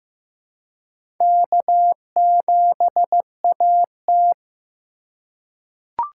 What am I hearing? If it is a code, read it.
Morse code, heard as a keyed tone, about 15 wpm, 700 Hz. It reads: K7AT
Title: USA Call Signs - 4 Characters - 3x Character Spacing - 15wpm